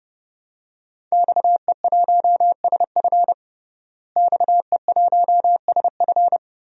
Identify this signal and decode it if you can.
Morse code, heard as a keyed tone, about 32 wpm, 700 Hz. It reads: XE1SF XE1SF
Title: DXCC-100 Call Signs - 5 Characters - Repeated 3x - 30wpm